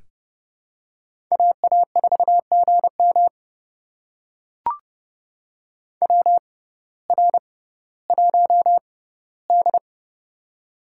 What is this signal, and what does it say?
Morse code, heard as a keyed tone, about 30 wpm, spaced as for 17 wpm, 700 Hz. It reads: AA4GM WR1D
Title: USA Call Signs - 6x Character Spacing - 30wpm